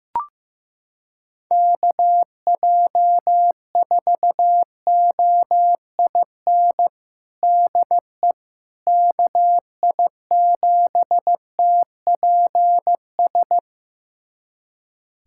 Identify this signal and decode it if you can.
Morse code, heard as a keyed tone, about 15 wpm, 700 Hz. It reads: KJ4OIN DE KI7TPS
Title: USA Callsign DE Callsign 15wpm